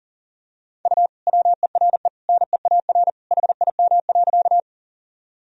Morse code, heard as a keyed tone, about 40 wpm, 700 Hz. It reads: U WERE NEAR HIM.
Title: Sentences from Top 200 Words and Abbreviations 40wpm